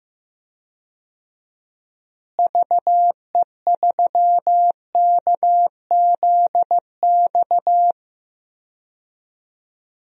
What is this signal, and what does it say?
Morse code, heard as a keyed tone, about 15 wpm, 700 Hz. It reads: VE3KZX
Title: DXCC-100 Call Signs - 6 Characters - Repeated 3x - 15wpm